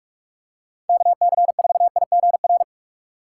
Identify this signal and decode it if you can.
Morse code, heard as a keyed tone, about 46 wpm, 700 Hz. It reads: KC4IGR
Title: USA Call Signs - 4x Character Spacing - 45wpm